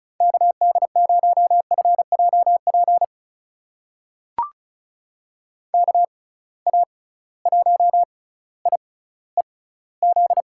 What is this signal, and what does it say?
Morse code, heard as a keyed tone, about 34 wpm, spaced as for 18 wpm, 700 Hz. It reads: KD0FJP KA1IEZ
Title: USA Call Signs - 6 Characters - 6x Character Spacing - 35wpm